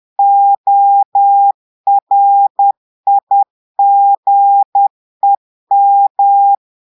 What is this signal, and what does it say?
Morse code, heard as a keyed tone, about 10 wpm, 800 Hz. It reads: ORIGEM
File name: Portugais - texte en clair 1 - vitesse - 10 mots minute